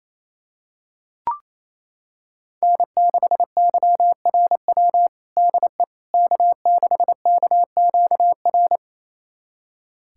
Morse code, heard as a keyed tone, about 28 wpm, 700 Hz. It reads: N6YRW DE K6KQR
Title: USA Callsign DE Callsign - 5 Characters - 28wpm